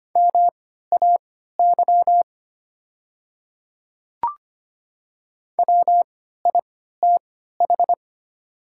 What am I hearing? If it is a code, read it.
Morse code, heard as a keyed tone, about 25 wpm, spaced as for 14 wpm, 700 Hz. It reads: MAY WITH
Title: 3x Character Spacing - Top 500 Words - 25wpm